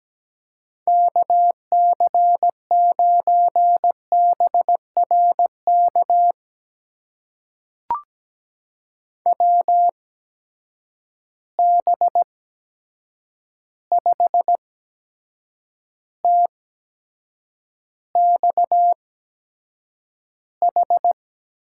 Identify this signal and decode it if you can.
Morse code, heard as a keyed tone, about 17 wpm, spaced as for 7 wpm, 700 Hz. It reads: KC9BRK WB5TXH
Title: USA Call Signs - 6 Characters - 8x Character Spacing - 17wpm